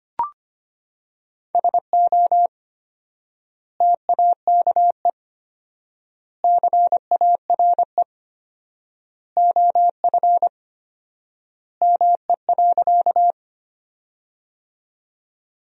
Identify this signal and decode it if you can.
Morse code, heard as a keyed tone, about 25 wpm, 700 Hz. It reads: SO TAKE CARE OF ME.
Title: 4x Word-Spacing - Sentences from Top 300 Words - 25wpm